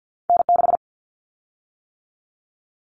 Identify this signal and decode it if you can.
Morse code, heard as a keyed tone, about 50 wpm, 700 Hz. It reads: N6
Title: Callsign Prefixes for Top 25 DXCC - 50wpm